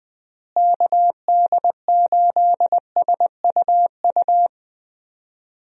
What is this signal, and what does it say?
Morse code, heard as a keyed tone, about 20 wpm, 700 Hz. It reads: KD8SUU
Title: USA Call Signs - 6 Characters - Repeated 2x - 20wpm